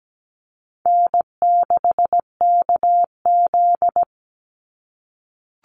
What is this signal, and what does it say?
Morse code, heard as a keyed tone, about 17 wpm, 700 Hz. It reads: N6KZ
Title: NA SOTA Chaser Callsigns 17wpm